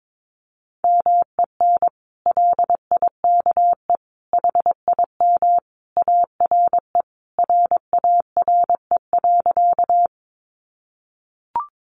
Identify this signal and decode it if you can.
Morse code, heard as a keyed tone, about 22 wpm, 700 Hz. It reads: MEN LIKE HIM ARE RARE.
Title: Sentences from Top 1100 Words 22wpm